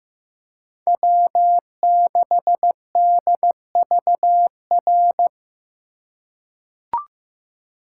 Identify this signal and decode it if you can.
Morse code, heard as a keyed tone, about 15 wpm, 700 Hz. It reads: W6DVR
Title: USA Call Signs - 4x Character Spacing - 15wpm